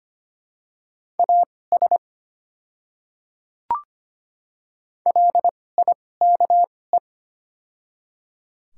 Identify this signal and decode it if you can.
Morse code, heard as a keyed tone, about 25 wpm, spaced as for 18 wpm, 700 Hz. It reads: AS LIKE
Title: 2x Character Spacing - Top 100 Words - 25wpm